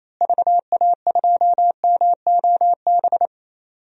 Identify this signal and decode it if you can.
Morse code, heard as a keyed tone, about 28 wpm, 700 Hz. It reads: VA2MOB
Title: DXCC-100 Call Signs - 6 Characters - Repeated 3x - 28wpm